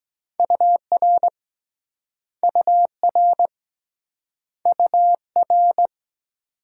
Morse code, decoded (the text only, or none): UR UR UR